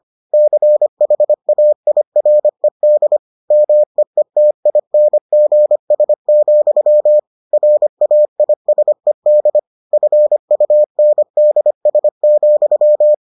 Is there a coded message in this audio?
Yes: Morse code from 0.3 to 13.1 s, heard as a keyed tone, about 25 wpm, 600 Hz. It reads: CHAIRED MEETINGS, RAISED FUNDS,